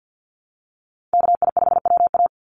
Morse code, heard as a keyed tone, about 50 wpm, 700 Hz. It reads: KI5WA
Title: NA SOTA Chaser Callsigns 50wpm